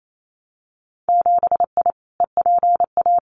Morse code, heard as a keyed tone, about 28 wpm, 700 Hz. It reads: 7I EPA
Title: Winter Field Day Exchanges Encore 28wpm